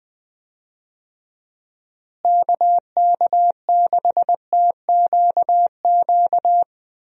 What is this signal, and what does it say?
Morse code, heard as a keyed tone, about 20 wpm, 700 Hz. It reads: KK6TQQ
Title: USA Call Signs - 6 Characters - Repeated 3x - 20wpm